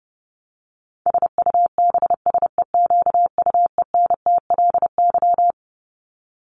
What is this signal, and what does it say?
Morse code, heard as a keyed tone, about 30 wpm, 700 Hz. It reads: SUBSEQUENTLY
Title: LY Suffix - Top 100 Words with Base Words 30wpm